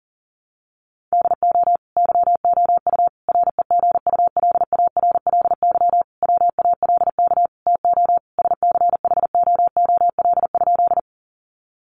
Morse code, heard as a keyed tone, about 40 wpm, 700 Hz. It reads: DO YOU REGULARLY WALK TO SCHOOL?
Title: Sentences from Top 1400 Words 40wpm